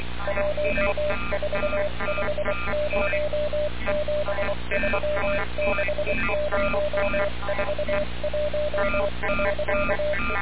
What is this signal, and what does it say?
Morse code, heard as a keyed tone, about 24 wpm, spaced as for 21 wpm, 600 Hz, with noise. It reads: 1450736041.
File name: radiocap / REC z